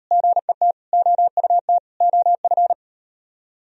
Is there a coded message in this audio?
Yes: Morse code from 0.1 to 2.7 s, heard as a keyed tone, about 34 wpm, 700 Hz. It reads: GET OUT OF